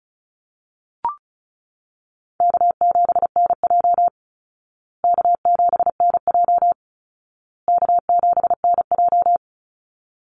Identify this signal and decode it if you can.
Morse code, heard as a keyed tone, about 34 wpm, 700 Hz. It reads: K7NJ K7NJ K7NJ
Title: Call Signs Repeated 2x - 35wpm